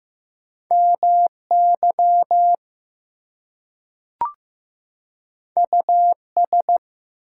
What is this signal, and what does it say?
Morse code, heard as a keyed tone, about 15 wpm, 700 Hz. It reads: MY US
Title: Two Letter Words - 15wpm